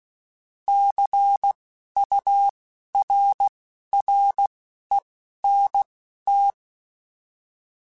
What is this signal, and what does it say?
Morse code, heard as a keyed tone, about 16 wpm, spaced as for 12 wpm, 800 Hz. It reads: CURRENT
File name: Technical Subjects / Technical Subjects-mp3s / XL @10